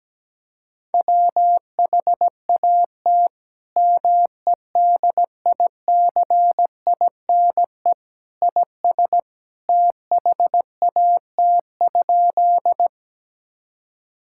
Morse code, heard as a keyed tone, about 17 wpm, 700 Hz. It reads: WHAT MEDICINE IS THAT?
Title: Sentences from Top 1000 Words - Review Part 6 - 17wpm